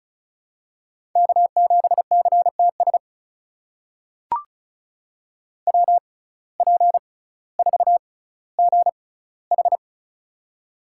Morse code, heard as a keyed tone, about 36 wpm, spaced as for 18 wpm, 700 Hz. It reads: K7CTS WP4GH
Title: USA Call Signs - 5 Characters - 6x Character Spacing - 35wpm